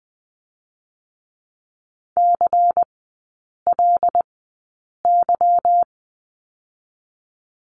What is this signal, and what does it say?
Morse code, heard as a keyed tone, about 20 wpm, 690 Hz. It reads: C L Y
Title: 145 Lesson 24e - Letter y - Three Characters 20wpm